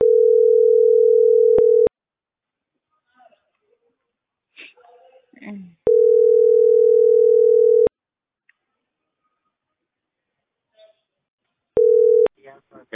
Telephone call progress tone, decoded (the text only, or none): ringback tone